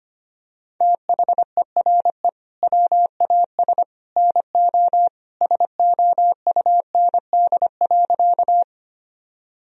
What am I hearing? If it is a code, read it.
Morse code, heard as a keyed tone, about 25 wpm, 700 Hz. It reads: THERE WAS NO SOUND.
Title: Sentences from Top 1000 Words - Review Part 1 - 25wpm